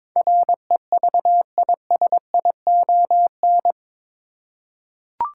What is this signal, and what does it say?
Morse code, heard as a keyed tone, about 22 wpm, 700 Hz. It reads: REVISION
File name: practice / 22wpm / Eight Letter Words - 22wpm